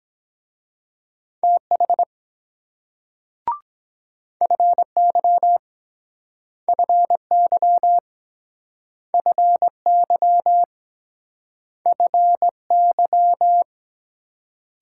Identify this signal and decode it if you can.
Morse code, heard as a keyed tone, about 22 wpm, 700 Hz. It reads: TH FY FY FY FY